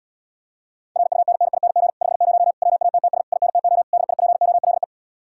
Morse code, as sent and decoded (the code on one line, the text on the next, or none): .-. ..- -. -. .. -. --. / ..... ----- / .-- .- - - ... / .. -. - --- / -.. .. .--. --- .-.. .
RUNNING 50 WATTS INTO DIPOLE